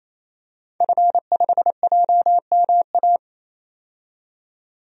Morse code, heard as a keyed tone, about 28 wpm, 700 Hz. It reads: F5JMA
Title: DXCC-100 Call Signs - 5 Characters - 28wpm